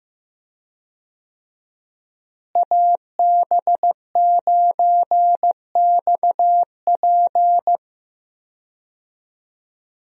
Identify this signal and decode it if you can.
Morse code, heard as a keyed tone, about 15 wpm, 700 Hz. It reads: AB9XP
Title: USA Call Signs - 5 Characters - Repeated 2x - 15wpm